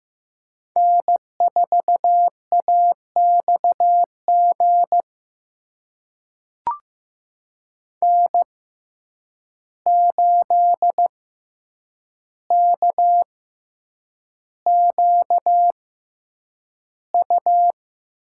Morse code, heard as a keyed tone, about 15 wpm, spaced as for 8 wpm, 700 Hz. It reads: N4AXG N8KQU
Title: USA Call Signs - 5 Characters - 6x Character Spacing - 15wpm